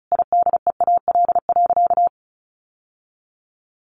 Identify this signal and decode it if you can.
Morse code, heard as a keyed tone, about 34 wpm, 700 Hz. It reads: IDEAL.
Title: Sentences from Top 1400 Words 35wpm